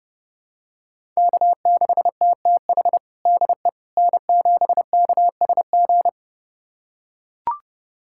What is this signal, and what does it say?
Morse code, heard as a keyed tone, about 30 wpm, 700 Hz. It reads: K6TTH DE N7KSG
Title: USA Callsign DE Callsign - 5 Characters - 30wpm